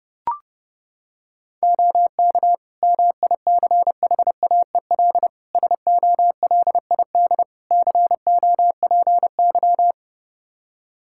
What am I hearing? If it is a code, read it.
Morse code, heard as a keyed tone, about 30 wpm, 700 Hz. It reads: OK MICHAEL SOLID COPY
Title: K1USN SST QSO Practice 30wpm